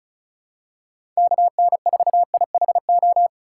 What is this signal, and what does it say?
Morse code, heard as a keyed tone, about 36 wpm, 700 Hz. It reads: KN4IHO
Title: USA Call Signs - 6 Characters - 4x Character Spacing - 35wpm